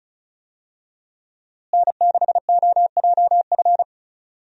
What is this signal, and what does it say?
Morse code, heard as a keyed tone, about 34 wpm, 700 Hz. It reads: N6OJF
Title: USA Call Signs - 5 Characters - Repeated 2x - 35wpm